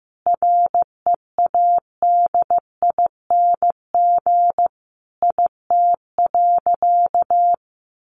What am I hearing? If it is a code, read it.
Morse code, heard as a keyed tone, about 15 wpm, 700 Hz. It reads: READING IT.
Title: Sentences from Top 1200 Words 15wpm